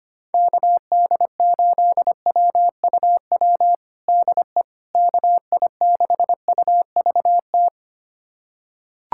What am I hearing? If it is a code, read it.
Morse code, heard as a keyed tone, about 25 wpm, 700 Hz. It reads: KD8WUW DE KI6UVT
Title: USA Callsign DE Callsign - 6 Characters - 25wpm